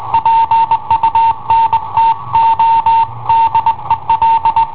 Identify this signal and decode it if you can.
Morse code, heard as a keyed tone, about 21 wpm, 910 Hz. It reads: PUNTO DEL
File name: i7ohp / cassetta i7ohp B